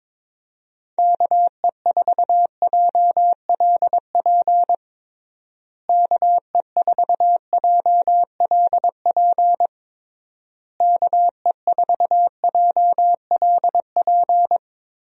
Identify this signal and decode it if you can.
Morse code, heard as a keyed tone, about 21 wpm, 700 Hz. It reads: KE4JLP KE4JLP KE4JLP
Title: USA Call Signs - 6 Characters - Repeated 3x - 22wpm